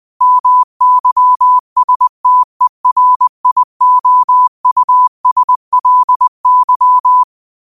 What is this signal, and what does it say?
Morse code, heard as a keyed tone, about 20 wpm, 1000 Hz. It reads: MYSTERIOUSLY
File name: qotdfiles / 20wpm